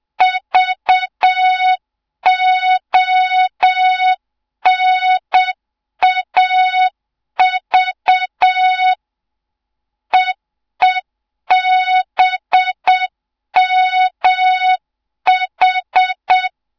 Morse code, heard as a keyed tone, about 7 wpm, 3700 Hz. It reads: VONAV EEBMH
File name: Morze-HA2MN / hang / 07 GYAKORLAT